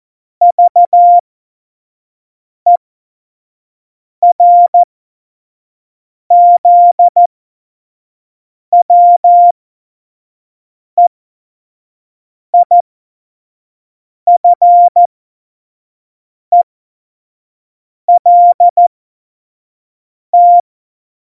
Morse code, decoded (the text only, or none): VERZWEIFELT